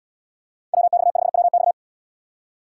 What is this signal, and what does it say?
Morse code, heard as a keyed tone, about 77 wpm, 700 Hz. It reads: WX5WX